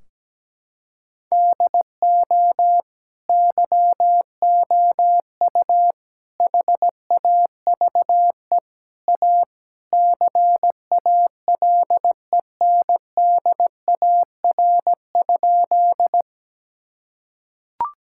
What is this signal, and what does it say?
Morse code, heard as a keyed tone, about 17 wpm, 700 Hz. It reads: DO YOU HAVE A CALENDAR?
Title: Sentences from Top 1600 Words 17wpm